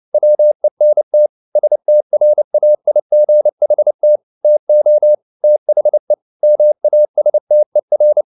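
Morse code, heard as a keyed tone, about 29 wpm, 600 Hz. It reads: WENT STRAIGHT TO THE MASTER
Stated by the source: Morse code at 29 wpm, sending Ken Follett's novel Never in English, from CW med Gnister.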